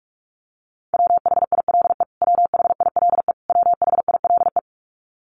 Morse code, heard as a keyed tone, about 45 wpm, 700 Hz. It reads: WHILE WHILE WHILE
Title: Top 500 Words Repeated 45wpm